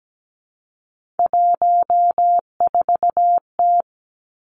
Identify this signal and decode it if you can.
Morse code, heard as a keyed tone, about 17 wpm, 700 Hz. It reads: 14T
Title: Three Challenging Letter-Number Combinations 17wpm